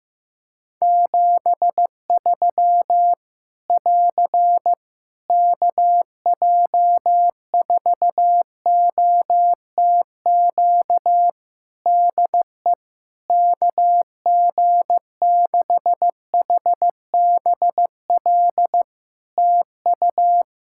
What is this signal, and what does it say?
Morse code, heard as a keyed tone, about 15 wpm, 700 Hz. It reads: 73 <AR> KJ4OTQ DE KG6HBL TU
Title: Practice QSO - Exchange 4 - Ending - 15wpm